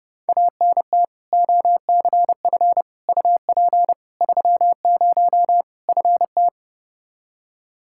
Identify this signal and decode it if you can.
Morse code, heard as a keyed tone, about 32 wpm, 700 Hz. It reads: ANT OCF UP 30 FT
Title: Practice QSO - Exchange 2 - Info - 30wpm